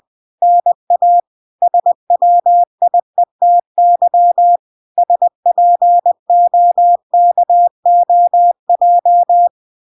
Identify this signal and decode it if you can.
Morse code, heard as a keyed tone, about 20 wpm, 700 Hz. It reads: NA SWIETY SPOKOJ